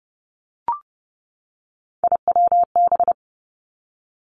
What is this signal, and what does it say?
Morse code, heard as a keyed tone, about 30 wpm, 700 Hz. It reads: IWB
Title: Three Challenging Letter-Number Combinations 30wpm